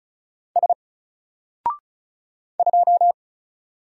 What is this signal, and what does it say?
Morse code, heard as a keyed tone, about 33 wpm, 700 Hz. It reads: S 2
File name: Single-Letter-Number-35wpm